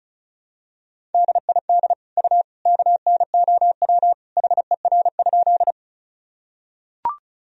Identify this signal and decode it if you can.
Morse code, heard as a keyed tone, about 36 wpm, 700 Hz. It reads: DID U KNOW HER?
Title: Sentences from Top 100 Words and Abbreviations 35wpm